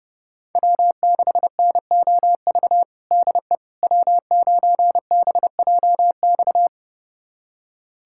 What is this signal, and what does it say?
Morse code, heard as a keyed tone, about 32 wpm, 700 Hz. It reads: W6NOV DE W9BJX
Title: USA Callsign DE Callsign - 5 Characters - 30wpm